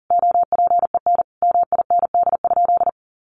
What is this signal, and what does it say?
Morse code, heard as a keyed tone, about 40 wpm, 700 Hz. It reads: OPEN MIND?
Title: Sentences from Top 1300 Words 40wpm